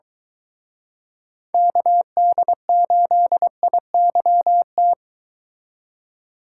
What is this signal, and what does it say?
Morse code, heard as a keyed tone, about 24 wpm, 700 Hz. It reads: KD8IYT